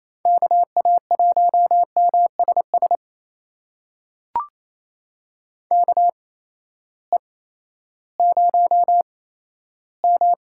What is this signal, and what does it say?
Morse code, heard as a keyed tone, about 28 wpm, spaced as for 14 wpm, 700 Hz. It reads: KA1MSS KE0M
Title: USA Call Signs - 6 Characters - 8x Character Spacing - 28wpm